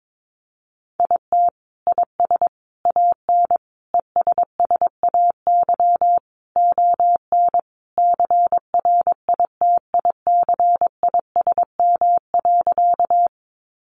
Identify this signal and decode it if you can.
Morse code, heard as a keyed tone, about 22 wpm, 700 Hz. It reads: IT IS AN ESSAY ON CRITICISM.
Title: Sentences from Top 1500 Words 22wpm